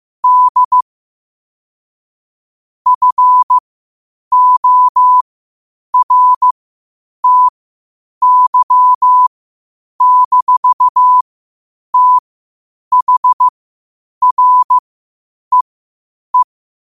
Morse code, heard as a keyed tone, about 15 wpm, spaced as for 8 wpm, 1000 Hz. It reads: D FORTY-THREE
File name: 8wpm